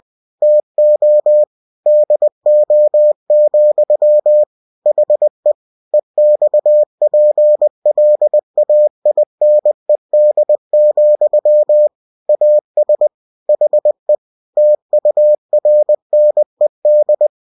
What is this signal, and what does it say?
Morse code, heard as a keyed tone, about 20 wpm, 600 Hz. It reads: TO DO, HE EXPLAINED, AS HE TURNED
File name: cw-books / winnie-the-pooh / winnie-20wpm-page-0003